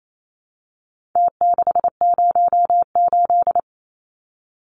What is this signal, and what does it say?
Morse code, heard as a keyed tone, about 28 wpm, 700 Hz. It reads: T608
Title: Numbers - Contest Style - 4-Digit 28wpm